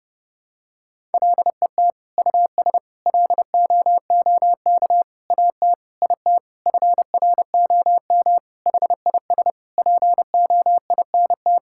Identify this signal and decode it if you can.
Morse code, heard as a keyed tone, about 32 wpm, 700 Hz. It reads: LET US LOOK AT IT FROM HIS POINT
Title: Sentences from Top 1000 Words - Review Part 6 - 30wpm